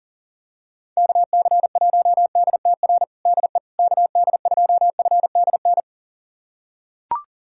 Morse code, heard as a keyed tone, about 37 wpm, 700 Hz. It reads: KC1DTR DE KD2FDN
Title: USA Callsign DE Callsign 40wpm